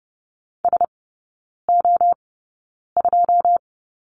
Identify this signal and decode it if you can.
Morse code, heard as a keyed tone, about 30 wpm, spaced as for 9 wpm, 690 Hz. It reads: SO2
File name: Lesson 16e - Letter c - Three Characters 30wpm